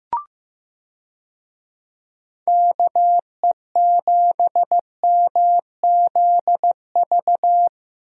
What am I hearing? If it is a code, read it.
Morse code, heard as a keyed tone, about 15 wpm, 700 Hz. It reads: KE7MZV
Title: USA Call Signs - 6 Characters - Repeated 2x - 15wpm